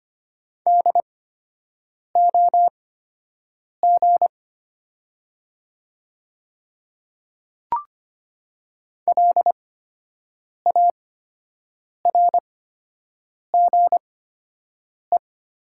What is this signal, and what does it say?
Morse code, heard as a keyed tone, about 25 wpm, spaced as for 7 wpm, 700 Hz. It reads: DOG LARGE
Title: 8x Character Spacing - Top 500 Words - 25wpm